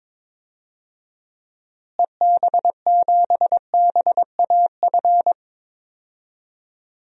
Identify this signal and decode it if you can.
Morse code, heard as a keyed tone, about 22 wpm, 700 Hz. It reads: EB7BAF
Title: DXCC-100 Call Signs - 6 Characters - Repeated 3x - 22wpm